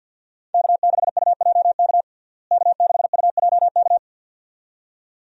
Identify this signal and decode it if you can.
Morse code, heard as a keyed tone, about 48 wpm, 700 Hz. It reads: K6UJK K6UJK
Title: USA Call Signs - 5 Characters - Repeated 2x - 50wpm